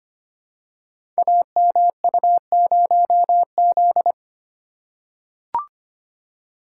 Morse code, heard as a keyed tone, about 25 wpm, 700 Hz. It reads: AMU0Z
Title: Code Groups - 5 Letter:Numbers - 25wpm